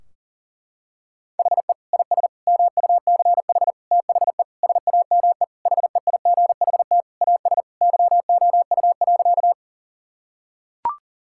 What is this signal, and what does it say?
Morse code, heard as a keyed tone, about 40 wpm, 700 Hz. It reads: HE IS MUCH THE SAME HEIGHT AS YOU.
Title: Sentences from Top 1600 Words 40wpm